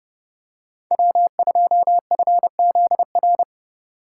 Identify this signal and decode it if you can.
Morse code, heard as a keyed tone, about 30 wpm, 700 Hz. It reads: W2FZR